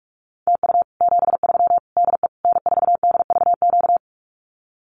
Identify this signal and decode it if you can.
Morse code, heard as a keyed tone, about 44 wpm, 700 Hz. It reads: TU 73 DE N4DVQ
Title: Common QSO Phrases Encore 45wpm